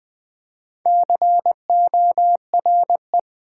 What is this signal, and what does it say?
Morse code, heard as a keyed tone, about 20 wpm, 690 Hz. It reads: CORE
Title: Build A Word - 1 Letter At A Time - 4-Letter Words - 20wpm